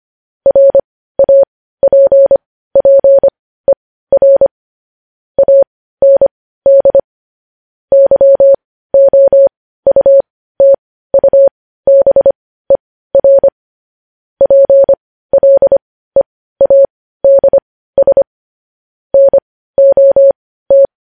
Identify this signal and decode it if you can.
Morse code, heard as a keyed tone, about 25 wpm, spaced as for 15 wpm, 550 Hz. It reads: RAPPER AND YOUTUBER PLEADS NOT
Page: News headlines in Morse code at 15 wpm